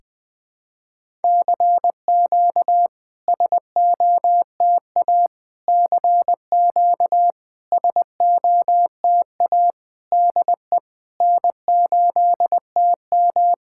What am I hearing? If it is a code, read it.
Morse code, heard as a keyed tone, about 20 wpm, 700 Hz. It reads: CQ SOTA CQ SOTA DE N8TM
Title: SOTA QSO Practice 20wpm